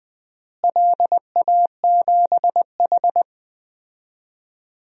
Morse code, heard as a keyed tone, about 20 wpm, 700 Hz. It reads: LA7H
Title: DXCC-100 Call Signs - 4 Characters - Repeated 3x - 20wpm